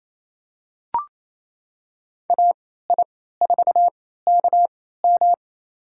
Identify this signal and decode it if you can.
Morse code, heard as a keyed tone, about 28 wpm, spaced as for 16 wpm, 700 Hz. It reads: AI4KM
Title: USA Call Signs - 3x Character Spacing - 28wpm